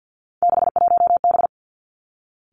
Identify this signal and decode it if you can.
Morse code, heard as a keyed tone, about 50 wpm, 700 Hz. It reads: 61B
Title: Three Challenging Letter-Number Combinations 50wpm